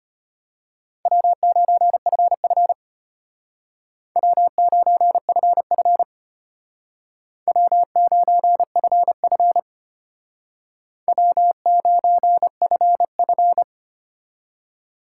Morse code, heard as a keyed tone, about 32 wpm, 700 Hz. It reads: W9FF W9FF W9FF W9FF